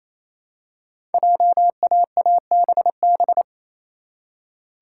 Morse code, heard as a keyed tone, about 27 wpm, 700 Hz. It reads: JAABB
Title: Code Groups - 5 Letters - 28wpm